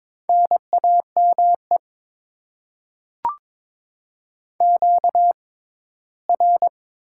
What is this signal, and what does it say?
Morse code, heard as a keyed tone, about 22 wpm, spaced as for 15 wpm, 700 Hz. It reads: NAME QR
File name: Top 90 QSO Elements - 6x Character Spacing - 22wpm